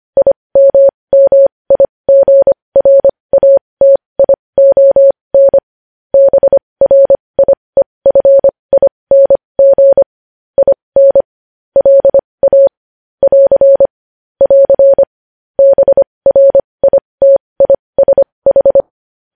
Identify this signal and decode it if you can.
Morse code, heard as a keyed tone, about 25 wpm, spaced as for 20 wpm, 550 Hz. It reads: IMMIGRATION BRIEFING IN LA <AR> <AR> BRITISH
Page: news-headlines-in-morse-code-at-20-wpm